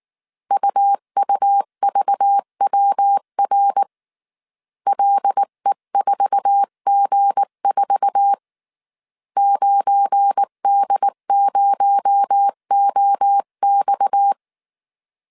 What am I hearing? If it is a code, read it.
Morse code, heard as a keyed tone, about 19 wpm, 800 Hz. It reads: UUVWR LE4G4 9D0OX
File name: CW / DET / 1 9 1500C S B 18wpm 2551